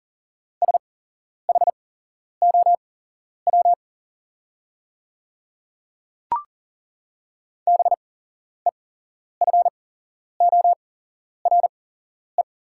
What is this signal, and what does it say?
Morse code, heard as a keyed tone, about 42 wpm, spaced as for 11 wpm, 700 Hz. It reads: SHOW BEFORE
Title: 8x Character Spacing - Top 200 Words - 40wpm